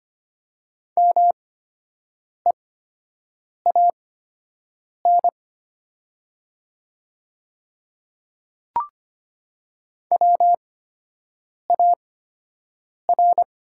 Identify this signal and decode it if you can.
Morse code, heard as a keyed tone, about 25 wpm, spaced as for 7 wpm, 700 Hz. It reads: MEAN WAR
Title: 8x Character Spacing - Top 500 Words - 25wpm